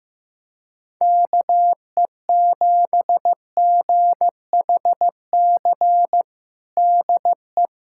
Morse code, heard as a keyed tone, about 15 wpm, 700 Hz. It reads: KE7GHC DE
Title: USA Callsign DE Callsign - 6 Characters - 15wpm